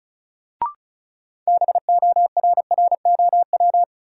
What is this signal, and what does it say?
Morse code, heard as a keyed tone, about 34 wpm, 700 Hz. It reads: BORROW
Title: Sentences from Top 1500 Words - New Word Drill Alt - 35wpm